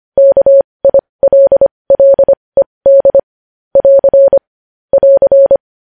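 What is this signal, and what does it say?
Morse code, heard as a keyed tone, about 25 wpm, spaced as for 20 wpm, 550 Hz. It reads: KILLED <AR> <AR>
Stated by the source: Morse practice sending news headlines at 20 wpm.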